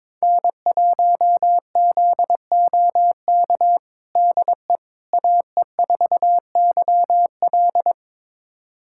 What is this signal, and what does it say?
Morse code, heard as a keyed tone, about 22 wpm, 700 Hz. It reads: N1ZOK DE AE4YL
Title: USA Callsign DE Callsign - 5 Characters - 22wpm